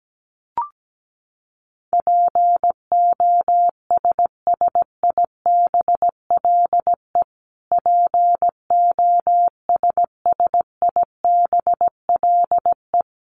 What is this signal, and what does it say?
Morse code, heard as a keyed tone, about 17 wpm, 700 Hz. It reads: POSSIBLE POSSIBLE
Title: Top 500 Words Repeated 17wpm